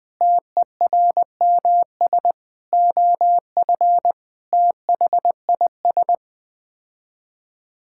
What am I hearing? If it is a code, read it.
Morse code, heard as a keyed tone, about 21 wpm, 700 Hz. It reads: TERMS OF THIS
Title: Sets of 9 Words - 20wpm